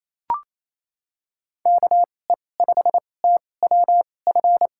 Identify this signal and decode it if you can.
Morse code, heard as a keyed tone, about 27 wpm, spaced as for 20 wpm, 700 Hz. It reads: KE5TWF
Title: USA Call Signs - 6 Characters - 2x Character Spacing - 28wpm